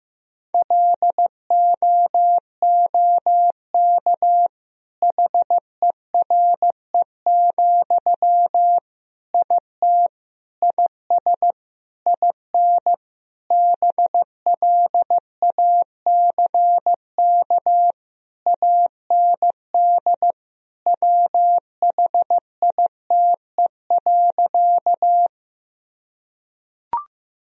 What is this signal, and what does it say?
Morse code, heard as a keyed tone, about 15 wpm, 700 Hz. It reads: LOOK HERE, IT IS IN BLACK AND WHITE.
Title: Sentences from Top 1000 Words - Review Part 2 - 15wpm